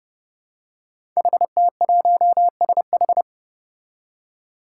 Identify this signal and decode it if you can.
Morse code, heard as a keyed tone, about 28 wpm, 700 Hz. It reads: HT1SH
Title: Five Challenging Letter-Number Combinations 30wpm